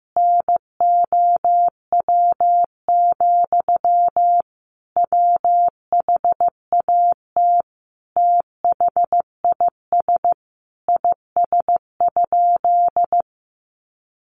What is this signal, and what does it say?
Morse code, heard as a keyed tone, about 14 wpm, 700 Hz. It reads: NOW, WHAT THIS IS?
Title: Sentences from Top 1500 Words 15wpm